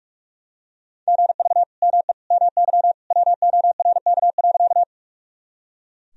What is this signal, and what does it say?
Morse code, heard as a keyed tone, about 46 wpm, 700 Hz. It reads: GV ME MY WORK.
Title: Sentences from Top 200 Words and Abbreviations 45wpm